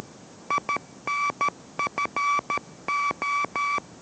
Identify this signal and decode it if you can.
Morse code, heard as a keyed tone, about 13 wpm, 1150 Hz. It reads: INFO